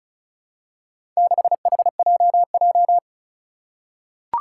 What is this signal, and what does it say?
Morse code, heard as a keyed tone, about 36 wpm, 700 Hz. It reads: 6HJJ
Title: Four Challenging Letter-Number Combinations 35wpm